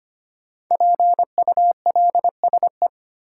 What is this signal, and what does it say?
Morse code, heard as a keyed tone, about 25 wpm, 700 Hz. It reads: PULSE
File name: 5 Letter Words - First 3 Letters Given - 25wpm